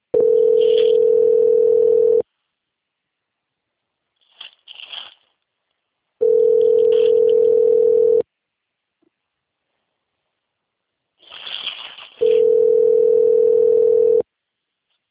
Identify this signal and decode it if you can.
Ringback tone (North America), 440 + 480 Hz.
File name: rg-1600-unknown- - - .12142